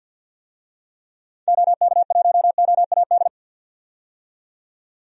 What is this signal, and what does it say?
Morse code, heard as a keyed tone, about 46 wpm, 700 Hz. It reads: OK1OAD
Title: DXCC-100 Call Signs - 6 Characters - 50wpm